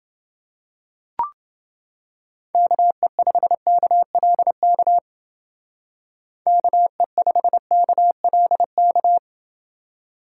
Morse code, heard as a keyed tone, about 29 wpm, 700 Hz. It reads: KE5KLK KE5KLK